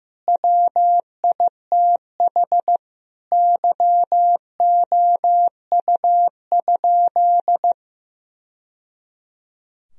Morse code, heard as a keyed tone, about 16 wpm, 700 Hz. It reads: WITH YOU?
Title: Sentences from Top 1000 Words - Review Part 2 - 15wpm